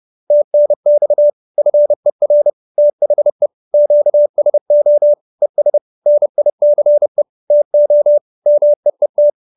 Morse code, heard as a keyed tone, about 30 wpm, 590 Hz. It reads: TNX FER THE QSO ES NICE TO MEET